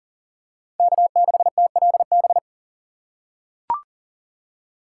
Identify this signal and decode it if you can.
Morse code, heard as a keyed tone, about 40 wpm, 700 Hz. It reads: K6TLB
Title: USA Call Signs - 5 Characters - 4x Character Spacing - 40wpm